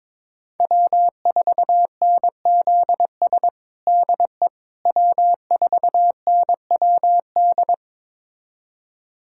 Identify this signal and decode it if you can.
Morse code, heard as a keyed tone, about 22 wpm, 700 Hz. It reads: W4NZS DE W4NWD
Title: USA Callsign DE Callsign - 5 Characters - 22wpm